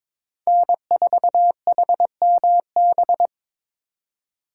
Morse code, heard as a keyed tone, about 26 wpm, 700 Hz. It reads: N4HMB
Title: Speed Racing - USA 1x3 Call Signs 22wpm